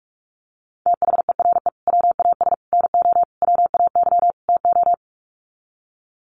Morse code, heard as a keyed tone, about 44 wpm, 700 Hz. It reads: THERE WAS NO WAY TO
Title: Sets of 5 Words Encore 45wpm